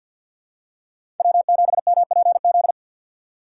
Morse code, heard as a keyed tone, about 52 wpm, 700 Hz. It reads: W7MPZ
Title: USA Call Signs - 5 Characters - 2x Character Spacing - 50wpm